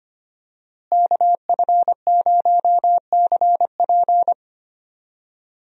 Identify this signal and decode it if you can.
Morse code, heard as a keyed tone, about 25 wpm, 700 Hz. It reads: KF0CP
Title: USA Call Signs - 5 Characters - Repeated 3x - 25wpm